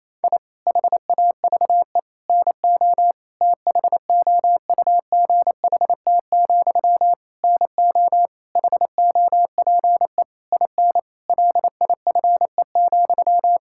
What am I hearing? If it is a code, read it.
Morse code, heard as a keyed tone, about 27 wpm, 700 Hz. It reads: I HAVE NO THOUGHT, NO HOPE IN LIFE,
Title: Sentences from Top 1000 Words - Review Part 6 - 28wpm